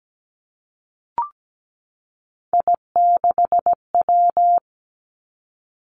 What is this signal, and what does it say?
Morse code, heard as a keyed tone, about 17 wpm, 700 Hz. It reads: I6W
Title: Three Challenging Letter-Number Combinations 17wpm